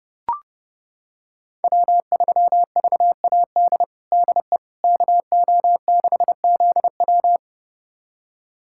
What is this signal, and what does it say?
Morse code, heard as a keyed tone, about 30 wpm, 700 Hz. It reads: W3VAD DE KO6ZW
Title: USA Callsign DE Callsign - 5 Characters - 30wpm